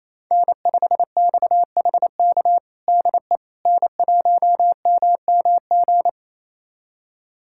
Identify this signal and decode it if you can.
Morse code, heard as a keyed tone, about 28 wpm, 700 Hz. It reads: N5XHK DE N1MMG
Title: USA Callsign DE Callsign - 5 Characters - 28wpm